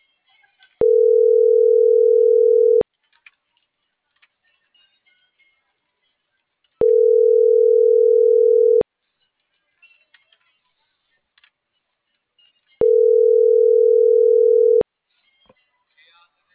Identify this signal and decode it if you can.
Ringback tone (North America), 440 + 480 Hz.